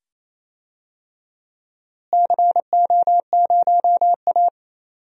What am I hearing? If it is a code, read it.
Morse code, heard as a keyed tone, about 28 wpm, 700 Hz. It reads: CO0A
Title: DXCC-100 Call Signs - 4 Characters - Repeated 2x - 28wpm